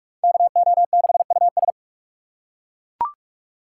Morse code, heard as a keyed tone, about 41 wpm, 700 Hz. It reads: KO6US
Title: USA Call Signs - 5 Characters - 4x Character Spacing - 45wpm